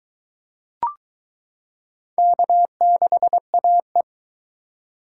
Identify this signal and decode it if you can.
Morse code, heard as a keyed tone, about 22 wpm, 700 Hz. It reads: K6AE